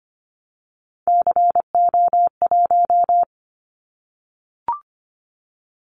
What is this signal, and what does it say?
Morse code, heard as a keyed tone, about 25 wpm, 700 Hz. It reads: CO1
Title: Callsign Prefixes for Top 100 DXCC - 25wpm